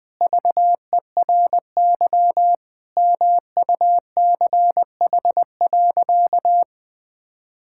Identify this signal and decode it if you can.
Morse code, heard as a keyed tone, about 20 wpm, 700 Hz. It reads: VERY MUCH.
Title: Sentences with 3-Word Phrases - Spoken Beginning - 20wpm